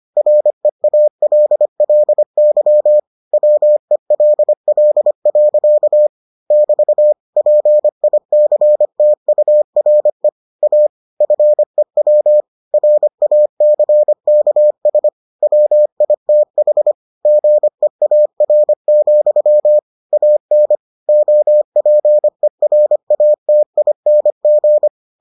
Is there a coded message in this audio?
Yes: Morse code from 0.2 to 24.9 s, heard as a keyed tone, about 25 wpm, 600 Hz. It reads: REALLY WELL. <BT> PICTURE A FEW RACKS WITH GEAR, AN OPERATING